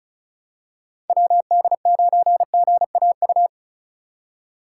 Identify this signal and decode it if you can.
Morse code, heard as a keyed tone, about 36 wpm, 700 Hz. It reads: WD9GAU
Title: USA Call Signs - 6 Characters - 4x Character Spacing - 35wpm